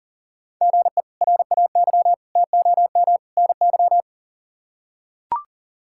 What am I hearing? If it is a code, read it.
Morse code, heard as a keyed tone, about 37 wpm, 700 Hz. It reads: GE RAY TOM NY
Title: K1USN SST QSO Practice - 40wpm